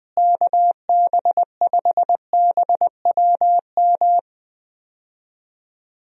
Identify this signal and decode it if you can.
Morse code, heard as a keyed tone, about 20 wpm, 700 Hz. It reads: KB5BWM
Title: USA Call Signs - 6 Characters - Repeated 2x - 20wpm